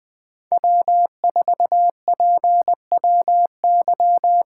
Morse code, heard as a keyed tone, about 21 wpm, 700 Hz. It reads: W4PWY
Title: USA Call Signs - 5 Characters - 3x Character Spacing - 20wpm